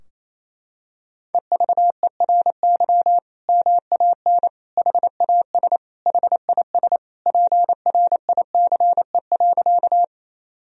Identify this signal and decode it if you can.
Morse code, heard as a keyed tone, about 28 wpm, 700 Hz. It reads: EVERY MAN HAS HIS PRICE.
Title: Sentences from Top 1000 Words - Review Part 4 - 28wpm